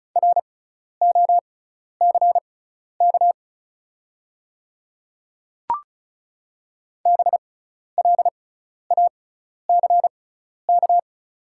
Morse code, heard as a keyed tone, about 36 wpm, spaced as for 12 wpm, 690 Hz. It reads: ROCK BLACK
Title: 6x Character Spacing - Top 500 Words - 35wpm